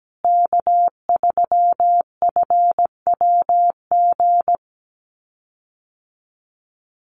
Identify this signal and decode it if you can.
Morse code, heard as a keyed tone, about 17 wpm, 700 Hz. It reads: K3FWG
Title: USA Call Signs Repeated 3x 17wpm